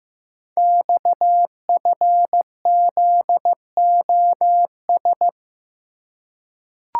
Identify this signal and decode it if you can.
Morse code, heard as a keyed tone, about 15 wpm, 700 Hz. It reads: XFZOS
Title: Code Groups - 5 Letters - 15wpm